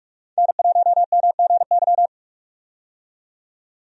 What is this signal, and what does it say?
Morse code, heard as a keyed tone, about 45 wpm, 700 Hz. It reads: N1MGY
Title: USA Call Signs - 5 Characters - 45wpm